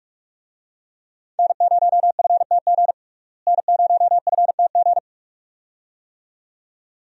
Morse code, heard as a keyed tone, about 44 wpm, 700 Hz. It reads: N0FTG N0FTG
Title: USA Call Signs - 5 Characters - Repeated 2x - 45wpm